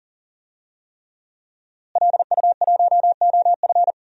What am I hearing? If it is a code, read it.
Morse code, heard as a keyed tone, about 38 wpm, 700 Hz. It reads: LU1OF
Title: DXCC-100 Call Signs - 5 Characters - Repeated 3x - 40wpm